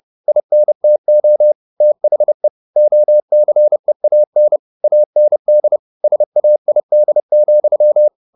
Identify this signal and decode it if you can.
Morse code, heard as a keyed tone, about 30 wpm, 600 Hz. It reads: INTO THE OCEAN AND SAID,